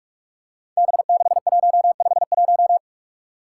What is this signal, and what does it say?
Morse code, heard as a keyed tone, about 46 wpm, 700 Hz. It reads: B6151
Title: Five Challenging Letter-Number Combinations 45wpm